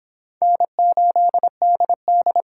Morse code, heard as a keyed tone, about 25 wpm, 700 Hz. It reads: N8DD